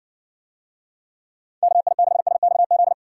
Morse code, heard as a keyed tone, about 60 wpm, 700 Hz. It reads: KI6SXZ